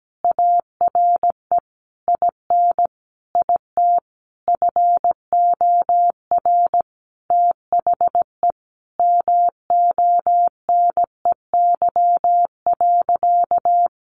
Morse code, heard as a keyed tone, about 17 wpm, 700 Hz. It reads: ARE IN IT FOR THE MONEY.
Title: Sentences from Top 1300 Words 17wpm